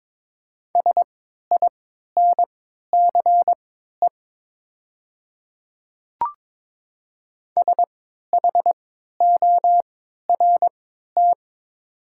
Morse code, heard as a keyed tone, about 22 wpm, spaced as for 13 wpm, 700 Hz. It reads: SINCE SHORT
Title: 3x Character Spacing - Top 500 Words - 22wpm